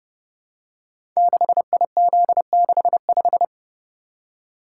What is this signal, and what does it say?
Morse code, heard as a keyed tone, about 30 wpm, 700 Hz. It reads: 6IZ65
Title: Code Groups - 5 Letter:Numbers - 30wpm